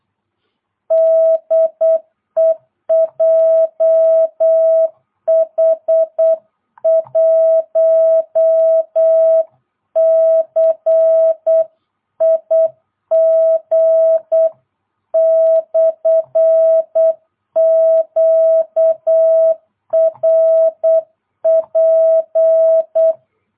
Morse code, heard as a keyed tone, about 8 wpm, 650 Hz. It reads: DE JH1CIG/QRP